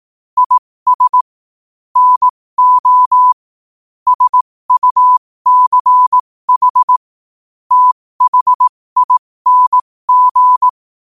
Morse code, heard as a keyed tone, about 18 wpm, spaced as for 15 wpm, 1000 Hz. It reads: IS NO SUCH THING
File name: qotdfiles / 15wpm